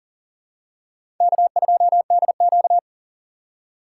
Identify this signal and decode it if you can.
Morse code, heard as a keyed tone, about 40 wpm, 700 Hz. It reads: K2DQ
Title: USA Call Signs - 4 Characters - 3x Character Spacing - 40wpm